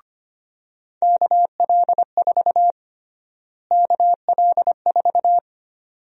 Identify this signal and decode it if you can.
Morse code, heard as a keyed tone, about 25 wpm, 700 Hz. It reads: KL4 KL4
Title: USA Call Sign Prefixes - Repeated 2x - 25wpm